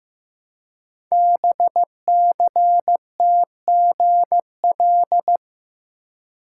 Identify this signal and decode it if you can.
Morse code, heard as a keyed tone, about 15 wpm, 700 Hz. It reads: BCTGL
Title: Code Groups - 5 Letters - 15wpm